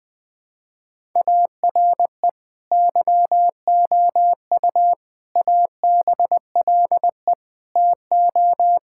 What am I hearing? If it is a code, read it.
Morse code, heard as a keyed tone, about 21 wpm, 700 Hz. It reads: ARE YOU ABLE TO
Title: Sentences from Top 1000 Words - Review Part 5 - 20wpm